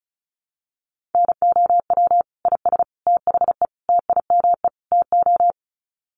Morse code, heard as a keyed tone, about 34 wpm, 700 Hz. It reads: NOW IS THE TIME TO
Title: Sets of 5 Words Encore2 35wpm